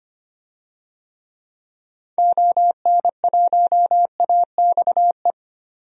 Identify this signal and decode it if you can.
Morse code, heard as a keyed tone, about 25 wpm, 700 Hz. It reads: ON1AXE
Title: DXCC-100 Call Signs - 6 Characters - 25wpm